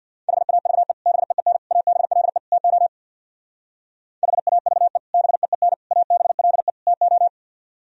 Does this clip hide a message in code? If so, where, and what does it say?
Yes: Morse code from 0.3 to 7.3 s, heard as a keyed tone, about 64 wpm, 700 Hz. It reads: HAVE BEEN ABLE TO HAVE BEEN ABLE TO